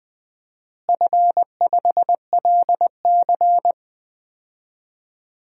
Morse code, heard as a keyed tone, about 20 wpm, 700 Hz. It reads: F5LC
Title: DXCC-100 Call Signs - 4 Characters - Repeated 3x - 20wpm